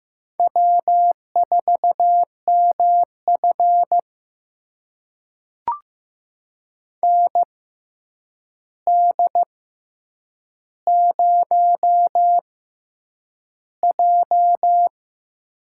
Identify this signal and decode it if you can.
Morse code, heard as a keyed tone, about 15 wpm, spaced as for 8 wpm, 700 Hz. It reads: W4MF ND0J
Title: USA Call Signs - 4 Characters - 6x Character Spacing - 15wpm